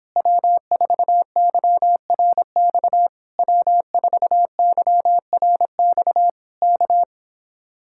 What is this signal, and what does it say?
Morse code, heard as a keyed tone, about 26 wpm, 700 Hz. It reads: W4YRX W4YRX K